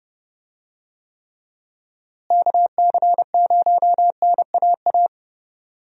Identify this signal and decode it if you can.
Morse code, heard as a keyed tone, about 30 wpm, 700 Hz. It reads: KC0NAA
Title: USA Call Signs - 6 Characters - Repeated 3x - 30wpm